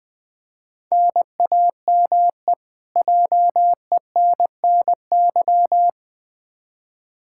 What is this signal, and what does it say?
Morse code, heard as a keyed tone, about 20 wpm, 700 Hz. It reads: NAME JENNY
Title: K1USN SST QSO Practice 20wpm